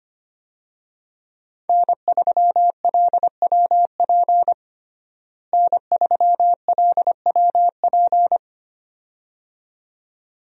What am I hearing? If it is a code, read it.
Morse code, heard as a keyed tone, about 25 wpm, 700 Hz. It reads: N3LWP N3LWP
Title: USA Call Signs - 5 Characters - Repeated 2x - 25wpm